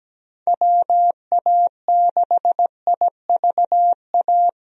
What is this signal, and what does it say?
Morse code, heard as a keyed tone, about 17 wpm, 700 Hz. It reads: WA6IVA
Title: USA Call Signs - 6 Characters - Repeated 3x - 17wpm